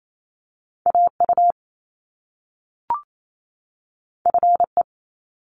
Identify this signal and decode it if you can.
Morse code, heard as a keyed tone, about 28 wpm, 700 Hz. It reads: AU FE
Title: Top Two Letter Combinations in Words 28wpm